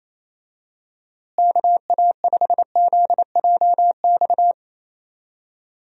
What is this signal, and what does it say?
Morse code, heard as a keyed tone, about 28 wpm, 700 Hz. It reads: KA5ZJX
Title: USA Call Signs - 6 Characters - Repeated 3x - 28wpm